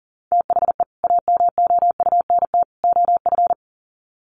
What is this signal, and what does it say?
Morse code, heard as a keyed tone, about 40 wpm, 700 Hz. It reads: THE AMOUNT OF
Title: Sets of 3 Words 40wpm